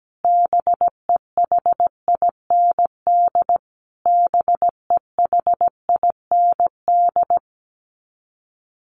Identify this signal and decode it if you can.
Morse code, heard as a keyed tone, about 17 wpm, 700 Hz. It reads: BEHIND BEHIND
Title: Top 500 Words Repeated 17wpm